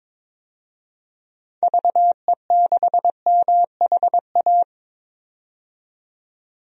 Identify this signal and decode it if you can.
Morse code, heard as a keyed tone, about 22 wpm, 700 Hz. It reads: VE6MHA
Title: DXCC-100 Call Signs - 6 Characters - 22wpm